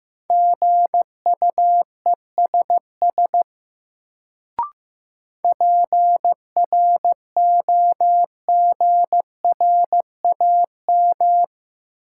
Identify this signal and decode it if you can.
Morse code, heard as a keyed tone, about 15 wpm, 700 Hz. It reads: GUESS PROGRAM
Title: Sentences from Top 1500 Words - New Word Drill Alt - 15wpm